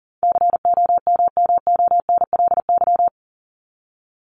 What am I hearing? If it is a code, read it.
Morse code, heard as a keyed tone, about 40 wpm, 700 Hz. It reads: COMMONLY
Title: LY Suffix - Top 100 Words with Base Words 40wpm